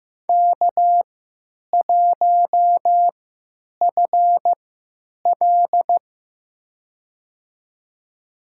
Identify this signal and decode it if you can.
Morse code, heard as a keyed tone, about 15 wpm, spaced as for 8 wpm, 700 Hz. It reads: K1FL
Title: USA Call Signs - 3x Character Spacing - 15wpm